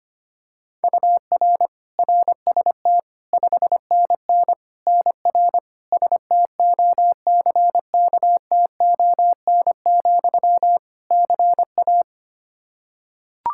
Morse code, heard as a keyed tone, about 25 wpm, 700 Hz. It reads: UR RST 5NN NR STOCKTON, CA
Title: Practice QSO - Exchange 1 - Beginning - 25wpm